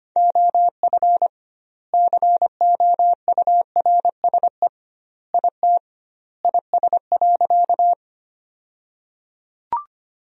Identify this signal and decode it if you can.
Morse code, heard as a keyed tone, about 25 wpm, 700 Hz. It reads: OF COURSE IT IS.
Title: 2x Word-Spacing - Sentences from Top 500 Words - 25wpm